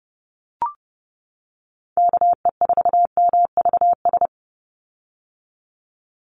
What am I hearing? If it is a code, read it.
Morse code, heard as a keyed tone, about 30 wpm, 700 Hz. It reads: KE4MVS
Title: USA Call Signs 30wpm